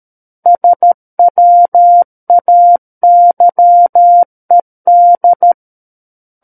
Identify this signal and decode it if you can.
Morse code, heard as a keyed tone, about 13 wpm, 700 Hz. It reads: SWAYED
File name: Book One / 04-Chapter 4 - 13 WPM